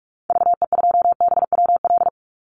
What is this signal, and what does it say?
Morse code, heard as a keyed tone, about 44 wpm, 700 Hz. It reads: VE2BWL
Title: Call Signs - 45wpm